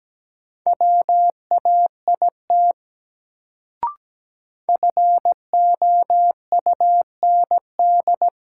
Morse code, heard as a keyed tone, about 17 wpm, 700 Hz. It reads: WAIT FOUND
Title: Sets of 5 Words Encore - New Word Drill - 17wpm